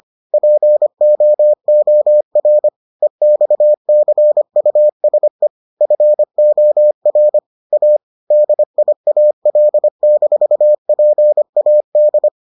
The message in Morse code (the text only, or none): POOR EXCUSE FOR A DIAL-PAD